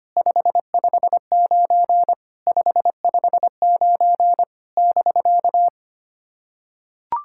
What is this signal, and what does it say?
Morse code, heard as a keyed tone, about 25 wpm, 700 Hz. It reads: 559 559 <BK>
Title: SOTA QSO Practice 25wpm